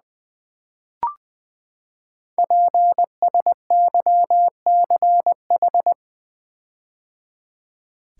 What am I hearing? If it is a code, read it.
Morse code, heard as a keyed tone, about 20 wpm, 700 Hz. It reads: PSYCH
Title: Build A Word - 5 Letters At A Time - 10-Letter Words - 20wpm